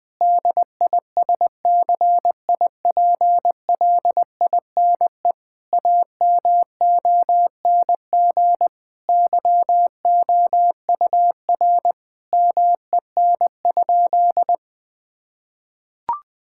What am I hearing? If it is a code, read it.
Morse code, heard as a keyed tone, about 21 wpm, 700 Hz. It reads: DISCIPLINE AMONG YOUR MEN?
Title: Sentences from Top 1000 Words - Review Part 6 - 20wpm